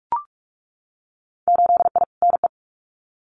Morse code, heard as a keyed tone, about 44 wpm, 700 Hz. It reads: 8I NE
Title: Winter Field Day Exchanges Encore 45wpm